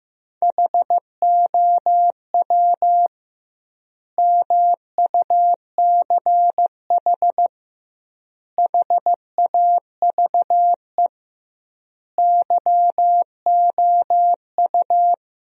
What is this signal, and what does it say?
Morse code, heard as a keyed tone, about 15 wpm, 700 Hz. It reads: HOW MUCH HAVE YOU
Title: 2x Word-Spacing - Sentences from Top 300 Words - 15wpm